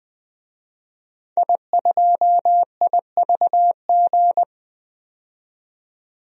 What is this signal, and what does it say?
Morse code, heard as a keyed tone, about 21 wpm, 700 Hz. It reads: I2IVG
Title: DXCC-100 Call Signs - 5 Characters - 20wpm